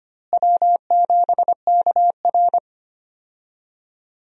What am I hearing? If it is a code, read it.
Morse code, heard as a keyed tone, about 25 wpm, 700 Hz. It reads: W7KR
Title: USA Call Signs - 4 Characters - Repeated 2x - 25wpm